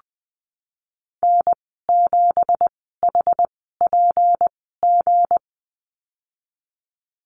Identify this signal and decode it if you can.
Morse code, heard as a keyed tone, about 20 wpm, spaced as for 14 wpm, 700 Hz. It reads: N7HPG
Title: 167 Lesson 27g - Number 7 - Call Signs 20wpm